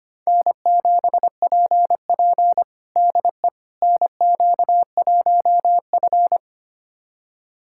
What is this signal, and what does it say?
Morse code, heard as a keyed tone, about 25 wpm, 700 Hz. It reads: N7PP DE NQ1F
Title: USA Callsign DE Callsign - 4 Characters - 25wpm